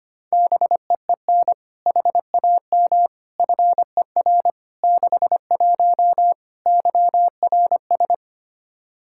Morse code, heard as a keyed tone, about 25 wpm, 700 Hz. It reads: BEEN HAM FER 61 YRS
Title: Practice QSO - Exchange 3 - More Info - 25wpm